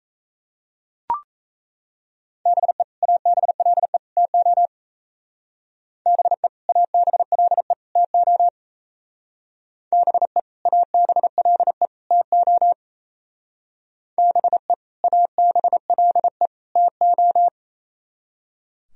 Morse code, heard as a keyed tone, about 35 wpm, 700 Hz. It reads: BE ABLE TO BE ABLE TO BE ABLE TO BE ABLE TO